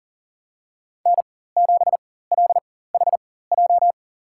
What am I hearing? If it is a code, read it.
Morse code, heard as a keyed tone, about 40 wpm, spaced as for 19 wpm, 700 Hz. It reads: N7LHJ
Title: USA Call Signs - 4x Character Spacing - 40wpm